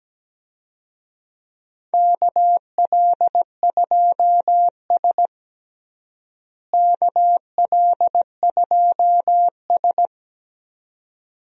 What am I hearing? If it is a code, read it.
Morse code, heard as a keyed tone, about 17 wpm, 700 Hz. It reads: KL2S KL2S
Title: USA Call Signs - 4 Characters - Repeated 2x - 17wpm